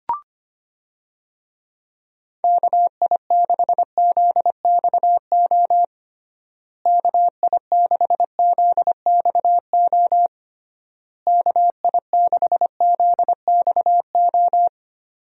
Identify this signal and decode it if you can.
Morse code, heard as a keyed tone, about 25 wpm, 700 Hz. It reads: KI6ZXO KI6ZXO KI6ZXO
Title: USA Call Signs - 6 Characters - Repeated 3x - 25wpm